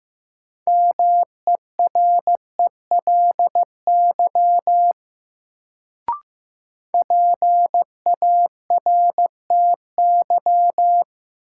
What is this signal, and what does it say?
Morse code, heard as a keyed tone, about 15 wpm, 700 Hz. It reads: MERELY PARTY
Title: Sentences from Top 1400 Words - New Word Drill - 15wpm